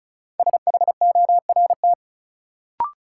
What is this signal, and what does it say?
Morse code, heard as a keyed tone, about 36 wpm, 700 Hz. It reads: SHORT
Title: Sentences from Top 400 Words - New Word Drill - 35wpm